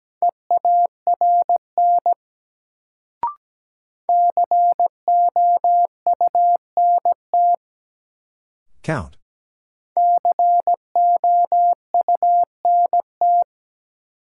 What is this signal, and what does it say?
Morse code, heard as a keyed tone, about 17 wpm, 700 Hz. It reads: EARN COUNT COUNT
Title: Sentences from Top 1500 Words - New Word Drill - 17wpm